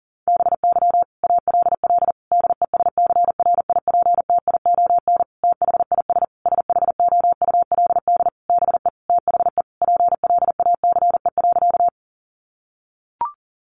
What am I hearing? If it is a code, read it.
Morse code, heard as a keyed tone, about 40 wpm, 700 Hz. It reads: BY ALL DESCRIPTION THIS SHOULD BE THE PLACE.
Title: Sentences from Top 1300 Words 40wpm